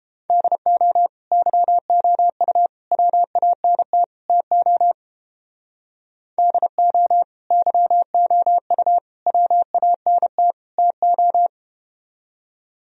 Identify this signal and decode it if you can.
Morse code, heard as a keyed tone, about 30 wpm, 700 Hz. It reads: DO YOU WANT TO DO YOU WANT TO